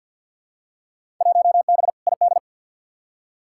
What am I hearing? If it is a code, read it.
Morse code, heard as a keyed tone, about 52 wpm, 700 Hz. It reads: 1B ID
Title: Field Day Exchanges - 50wpm